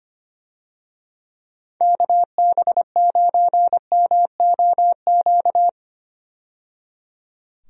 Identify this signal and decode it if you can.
Morse code, heard as a keyed tone, about 25 wpm, 700 Hz. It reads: KB9MOQ
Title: USA Call Signs - 6 Characters - 25wpm